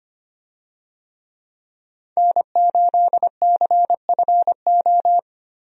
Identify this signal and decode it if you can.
Morse code, heard as a keyed tone, about 25 wpm, 700 Hz. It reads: N8CFO
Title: USA Call Signs - 5 Characters - 25wpm